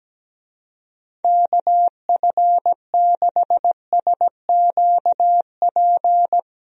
Morse code, heard as a keyed tone, about 17 wpm, 700 Hz. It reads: KF6SQP
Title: USA Call Signs - 6 Characters - Repeated 2x - 17wpm